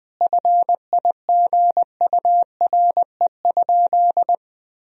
Morse code, heard as a keyed tone, about 20 wpm, 700 Hz. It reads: FIGURE?
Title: Sentences from Top 1000 Words - Review Part 2 - 20wpm